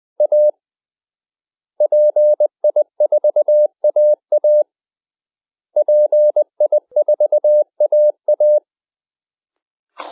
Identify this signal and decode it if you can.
Morse code, heard as a keyed tone, about 20 wpm, 600 Hz. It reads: A PI4AA PI4AA
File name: microgolf / 10ghz-websdr-eindhoven / pi4aa-uitzending -10ghzwebsdr